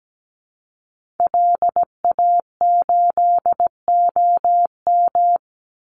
Morse code, heard as a keyed tone, about 17 wpm, 700 Hz. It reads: LA8OM
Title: Call Signs - 17wpm